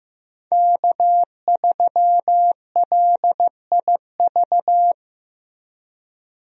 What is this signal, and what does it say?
Morse code, heard as a keyed tone, about 15 wpm, 700 Hz. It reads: K3LIV
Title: USA Call Signs - 5 Characters - Repeated 3x - 15wpm